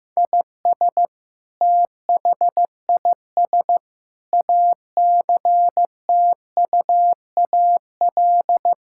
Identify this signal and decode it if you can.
Morse code, heard as a keyed tone, about 15 wpm, 700 Hz. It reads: IS THIS ACTUAL
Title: Sentences from Top 1000 Words - Review Part 5 - 15wpm